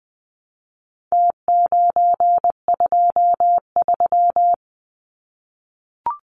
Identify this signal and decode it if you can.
Morse code, heard as a keyed tone, about 20 wpm, 700 Hz. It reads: T923
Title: Numbers - Contest Style - 4-Digit 20wpm